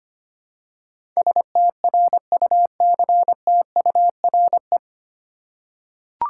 Morse code, heard as a keyed tone, about 25 wpm, 700 Hz. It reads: STRUCTURE
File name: practice / 25wpm / 9 Letter Words - First 3 Letters Given - 25wpm